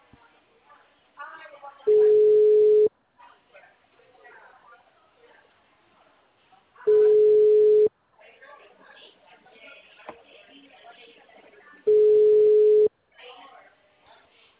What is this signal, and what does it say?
Ringback tone (Europe, 425 Hz), with speech.